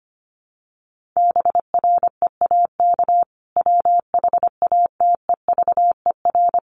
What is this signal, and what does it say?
Morse code, heard as a keyed tone, about 25 wpm, 700 Hz. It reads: BREAK WHATEVER
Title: Sentences from Top 1500 Words 25wpm